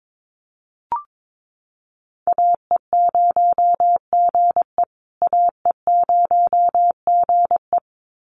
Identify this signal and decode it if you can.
Morse code, heard as a keyed tone, about 22 wpm, 700 Hz. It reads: AE0GE AE0GE
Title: USA Call Signs Repeated 2x 22wpm